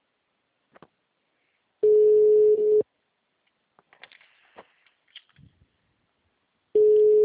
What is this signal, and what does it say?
Ringback tone (Europe, 425 Hz).
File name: out- -2002- - - .51232